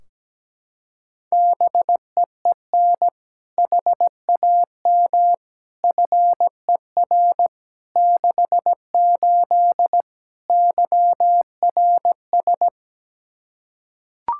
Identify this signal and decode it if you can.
Morse code, heard as a keyed tone, about 17 wpm, 700 Hz. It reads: BEEN HAM FER 68 YRS
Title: Practice QSO - Exchange 3 - More Info - 17wpm